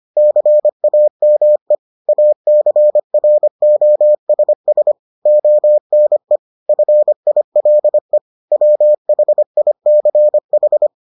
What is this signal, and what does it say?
Morse code, heard as a keyed tone, about 25 wpm, 600 Hz. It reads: CAME ACROSS ONE FILE WHICH